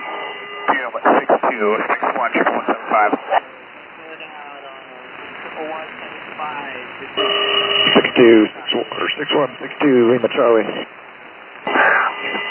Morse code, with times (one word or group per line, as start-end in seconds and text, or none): none